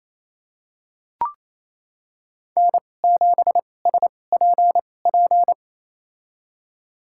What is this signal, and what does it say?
Morse code, heard as a keyed tone, about 28 wpm, spaced as for 20 wpm, 700 Hz. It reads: N7SPP
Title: USA Call Signs - 5 Characters - 2x Character Spacing - 28wpm